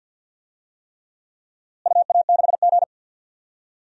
Morse code, heard as a keyed tone, about 46 wpm, 700 Hz. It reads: UA6G
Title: DXCC-100 Call Signs - 4 Characters - Repeated 3x - 50wpm